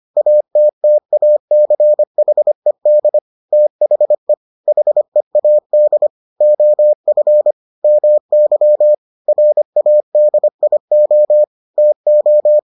Morse code, heard as a keyed tone, about 25 wpm, 600 Hz. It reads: ATTACHED THE HEAD OF MY RADIO TO